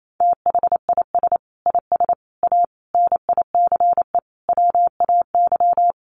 Morse code, heard as a keyed tone, about 28 wpm, 700 Hz. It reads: THIS IS A NICE WAY
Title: Sentences from Top 1100 Words 28wpm